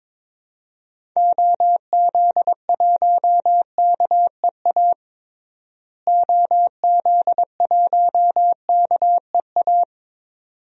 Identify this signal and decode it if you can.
Morse code, heard as a keyed tone, about 21 wpm, 700 Hz. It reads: OZ1KEA OZ1KEA
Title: DXCC-100 Call Signs - 6 Characters - Repeated 2x - 22wpm